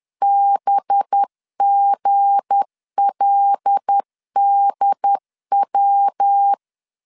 Morse code, heard as a keyed tone, about 10 wpm, 800 Hz. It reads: BGLDW
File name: LEZIONI / 9 286 10wpm 2735